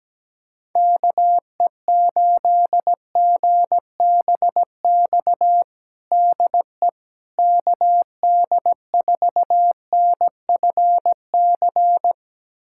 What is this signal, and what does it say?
Morse code, heard as a keyed tone, about 17 wpm, 700 Hz. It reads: KE8GBX DE KD4NFC
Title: USA Callsign DE Callsign - 6 Characters - 17wpm